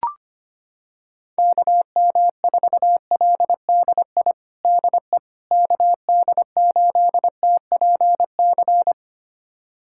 Morse code, heard as a keyed tone, about 25 wpm, 700 Hz. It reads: KM4LDI DE KD8TPC
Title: USA Callsign DE Callsign - 6 Characters - 25wpm